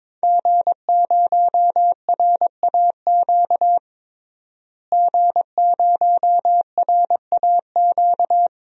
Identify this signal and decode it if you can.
Morse code, heard as a keyed tone, about 21 wpm, 700 Hz. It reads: G0RAQ G0RAQ
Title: DXCC-100 Call Signs - 5 Characters - Repeated 2x - 22wpm